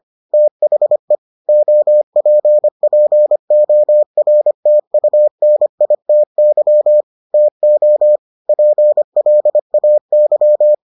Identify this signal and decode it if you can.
Morse code, heard as a keyed tone, about 25 wpm, 600 Hz. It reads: THE OPPORTUNITY TO PLAY